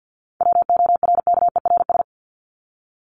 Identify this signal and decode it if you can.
Morse code, heard as a keyed tone, about 50 wpm, 700 Hz. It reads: WORKERS
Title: ES:S Suffix - Top 100 Words with Base Words 50wpm